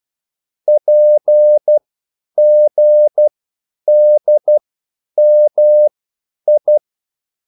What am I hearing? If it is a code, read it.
Morse code, heard as a keyed tone, about 12 wpm, spaced as for 9 wpm, 600 Hz. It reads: PGDMI
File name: QSL / CW lesson